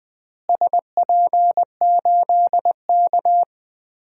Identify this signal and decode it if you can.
Morse code, heard as a keyed tone, about 20 wpm, 700 Hz. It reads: SP8K
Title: DXCC-100 Call Signs - 4 Characters - Repeated 3x - 20wpm